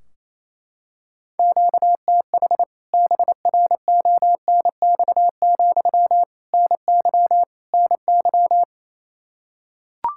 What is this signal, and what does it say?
Morse code, heard as a keyed tone, about 28 wpm, 700 Hz. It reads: QTH BRONX, NY NY
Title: K1USN SST QSO Practice 28wpm